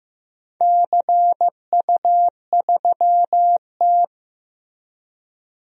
Morse code, heard as a keyed tone, about 15 wpm, 700 Hz. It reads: CU3T
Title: DXCC-100 Call Signs - 4 Characters - Repeated 3x - 15wpm